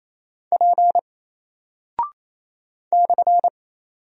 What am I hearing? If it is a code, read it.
Morse code, heard as a keyed tone, about 28 wpm, 700 Hz. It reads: P /
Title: Single Character (Letter:Number:Punctuation:Prosign) Alt 28wpm